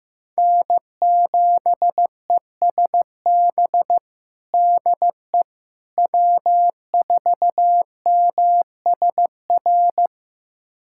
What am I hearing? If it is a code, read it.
Morse code, heard as a keyed tone, about 15 wpm, 700 Hz. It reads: N7ESB DE W4MSR
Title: USA Callsign DE Callsign - 5 Characters - 15wpm